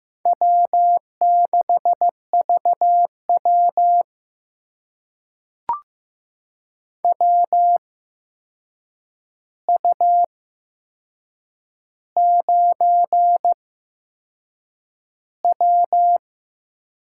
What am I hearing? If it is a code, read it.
Morse code, heard as a keyed tone, about 15 wpm, spaced as for 6 wpm, 700 Hz. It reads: W6VW WU9W
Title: USA Call Signs - 4 Characters - 8x Character Spacing - 15wpm